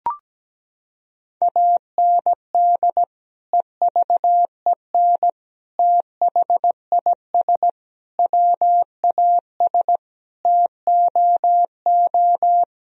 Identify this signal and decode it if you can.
Morse code, heard as a keyed tone, about 17 wpm, 700 Hz. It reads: AND EVEN THIS WAS TOO
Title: Sentences from Top 1000 Words - Review Part 1 - 17wpm